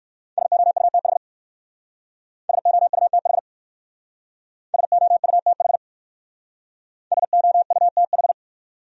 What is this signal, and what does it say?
Morse code, heard as a keyed tone, about 56 wpm, 700 Hz. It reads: SOUTH SOUTH SOUTH SOUTH
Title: Speed-Racing Top 500 Words 45wpm